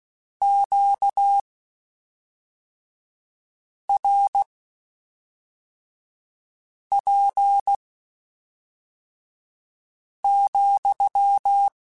Morse code, heard as a keyed tone, about 16 wpm, spaced as for 3 wpm, 800 Hz. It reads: QRP,
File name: Q Calls / Q Calls-mp3s / Q Calls SPSL2 @5